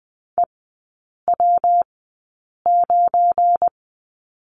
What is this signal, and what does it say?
Morse code, heard as a keyed tone, about 20 wpm, spaced as for 8 wpm, 690 Hz. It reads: EW9
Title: 172 Lesson 28e - Number 9 - Three Characters 20wpm